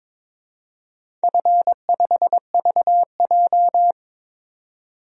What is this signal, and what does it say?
Morse code, heard as a keyed tone, about 22 wpm, 700 Hz. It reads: F5VJ
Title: DXCC-100 Call Signs - 4 Characters - 22wpm